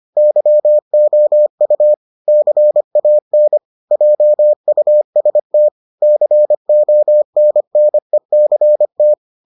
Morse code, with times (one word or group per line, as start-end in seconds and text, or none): 0.2-1.9 YOU
2.3-3.6 CAN
3.9-5.7 JUST
6.0-9.1 CONNECT